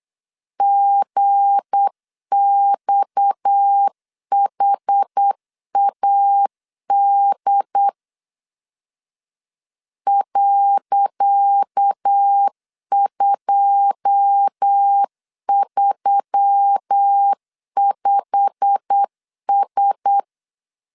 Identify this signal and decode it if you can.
Morse code, heard as a keyed tone, about 8 wpm, 800 Hz. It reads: GXHAD .235S
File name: LEZIONI / 10 SIGNS / 10C DOT 8wpm 3649